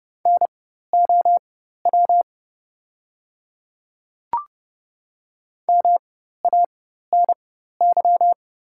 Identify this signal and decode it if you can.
Morse code, heard as a keyed tone, about 30 wpm, spaced as for 14 wpm, 700 Hz. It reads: NOW MANY
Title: 4x Character Spacing - Top 500 Words - 30wpm